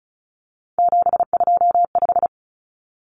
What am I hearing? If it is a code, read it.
Morse code, heard as a keyed tone, about 34 wpm, 700 Hz. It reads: 725
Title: Common Cut Numbers - 3-Digit 35wpm